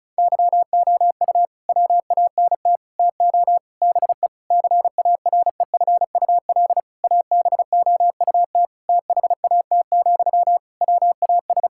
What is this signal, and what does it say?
Morse code, heard as a keyed tone, about 36 wpm, 700 Hz. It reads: YOU WANT TO BE CAREFUL ABOUT THAT, WAS
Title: Sentences from Top 1600 Words 35wpm